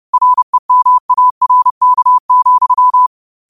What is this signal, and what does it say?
Morse code, heard as a keyed tone, about 30 wpm, 1000 Hz. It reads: REMARK,